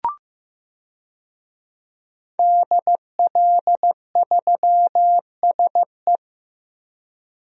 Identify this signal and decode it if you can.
Morse code, heard as a keyed tone, about 15 wpm, 700 Hz. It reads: DL3SE
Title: DXCC-100 Call Signs - 5 Characters - Repeated 2x - 15wpm